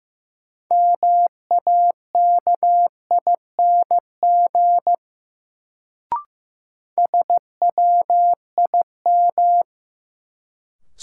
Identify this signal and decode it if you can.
Morse code, heard as a keyed tone, about 15 wpm, 700 Hz. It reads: MAKING SWIM
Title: Sentences from Top 1000 Words - Review Part 6 - New Word Drill - 15wpm